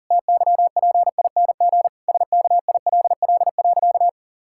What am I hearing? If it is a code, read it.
Morse code, heard as a keyed tone, about 40 wpm, 700 Hz. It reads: TYPING SKILL.
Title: Pangram Challenge v2 - 40wpm